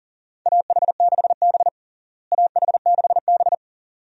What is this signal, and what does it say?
Morse code, heard as a keyed tone, about 40 wpm, 700 Hz. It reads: AH6B AH6B
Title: USA Call Signs - 4 Characters - Repeated 2x - 40wpm